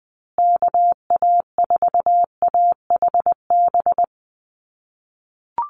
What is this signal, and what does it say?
Morse code, heard as a keyed tone, about 20 wpm, 700 Hz. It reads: KA4AHB
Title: USA Call Signs Repeated 2x 20wpm